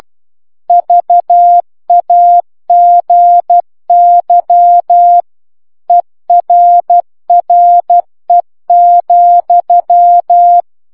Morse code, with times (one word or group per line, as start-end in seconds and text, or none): 0.7-5.2 VAGY
5.9-10.6 ERRE,